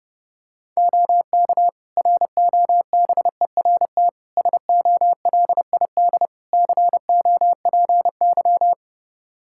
Morse code, heard as a keyed tone, about 32 wpm, 700 Hz. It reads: OK ROBERT SOLID COPY
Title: K1USN SST QSO Practice 30wpm